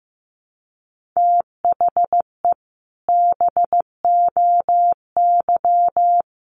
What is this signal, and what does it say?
Morse code, heard as a keyed tone, about 15 wpm, 700 Hz. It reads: THE BOY
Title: Sentences from Top 1500 Words 15wpm